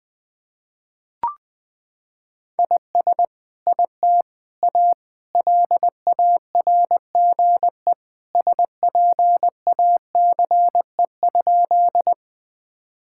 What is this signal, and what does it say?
Morse code, heard as a keyed tone, about 20 wpm, 700 Hz. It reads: IS IT A LARGE SPACE?
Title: Sentences from Top 1000 Words - Review Part 5 - 20wpm